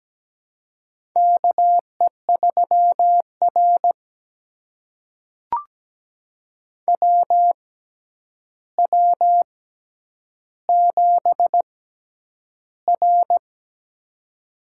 Morse code, heard as a keyed tone, about 17 wpm, spaced as for 9 wpm, 700 Hz. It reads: KE3R WW7R
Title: USA Call Signs - 4 Characters - 6x Character Spacing - 17wpm